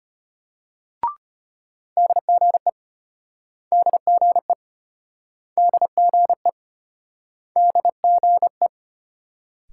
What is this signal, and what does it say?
Morse code, heard as a keyed tone, about 32 wpm, 700 Hz. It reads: DGE DGE DGE DGE